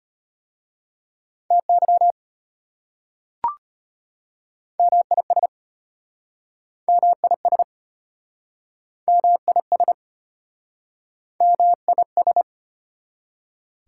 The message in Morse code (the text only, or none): TY MIS MIS MIS MIS